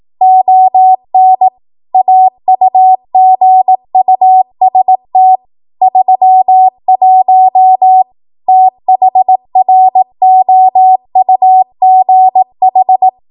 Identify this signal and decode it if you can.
Morse code, heard as a keyed tone, about 18 wpm, 750 Hz. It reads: ON AUGUST 31 THROUGH
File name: Archive / 18 WPM / 18WPM